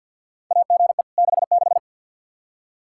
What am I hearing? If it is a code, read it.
Morse code, heard as a keyed tone, about 51 wpm, 700 Hz. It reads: AGE 66
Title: Practice QSO - Exchange 3 - More Info - 50wpm